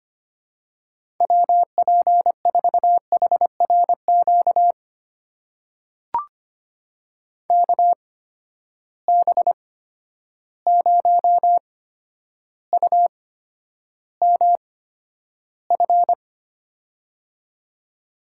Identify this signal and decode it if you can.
Morse code, heard as a keyed tone, about 25 wpm, spaced as for 11 wpm, 700 Hz. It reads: WP4HRQ KB0UMF
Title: USA Call Signs - 6 Characters - 8x Character Spacing - 25wpm